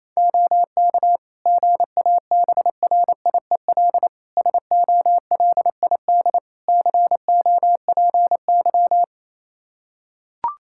Morse code, heard as a keyed tone, about 27 wpm, 700 Hz. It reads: OK GABRIEL SOLID COPY
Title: Practice QSO - Exchange 3 - More Info - 28wpm